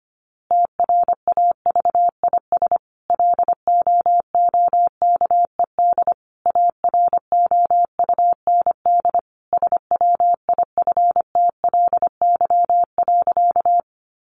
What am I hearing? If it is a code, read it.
Morse code, heard as a keyed tone, about 25 wpm, 700 Hz. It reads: TRAVIS LOOKED AROUND SWIFTLY.
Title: A Sound Of Thunder 25wpm